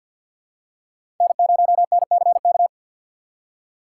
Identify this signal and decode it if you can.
Morse code, heard as a keyed tone, about 50 wpm, 700 Hz. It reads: N0NCK
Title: USA Call Signs - 6x Character Spacing - 50wpm